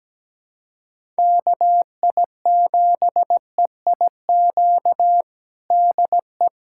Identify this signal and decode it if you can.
Morse code, heard as a keyed tone, about 17 wpm, 700 Hz. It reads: KI7EIQ DE
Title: USA Callsign DE Callsign 17wpm